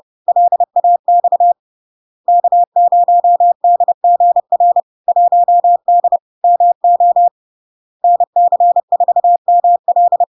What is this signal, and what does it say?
Morse code, heard as a keyed tone, about 30 wpm, 700 Hz. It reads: LAX K0DGR 1D MO NC4ML
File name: LARK-FD-2022-Log-in-CW-30wpm-700-Hz